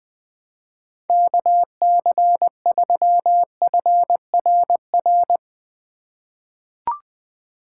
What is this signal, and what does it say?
Morse code, heard as a keyed tone, about 20 wpm, 700 Hz. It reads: KC3FRR
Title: USA Call Signs - 6 Characters - 2x Character Spacing - 20wpm